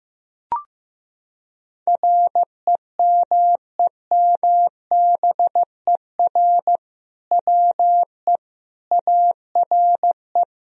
Morse code, heard as a keyed tone, about 16 wpm, 700 Hz. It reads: REMEMBER WE ARE
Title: Sentences from Top 1000 Words - Review Part 3 - 15wpm